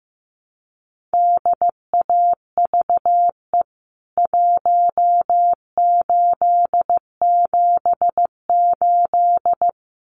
Morse code, heard as a keyed tone, about 15 wpm, 700 Hz. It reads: DAVE 1878
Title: CWT Exchange 15wpm